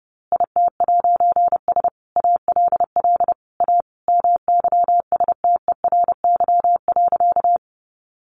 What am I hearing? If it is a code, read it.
Morse code, heard as a keyed tone, about 30 wpm, 700 Hz. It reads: IT'S ALL A MYSTERY.
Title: A Sound Of Thunder 30wpm